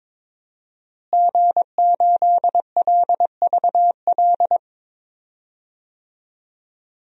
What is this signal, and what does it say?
Morse code, heard as a keyed tone, about 22 wpm, 700 Hz. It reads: G8LVL
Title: DXCC-100 Call Signs - 5 Characters - Repeated 3x - 22wpm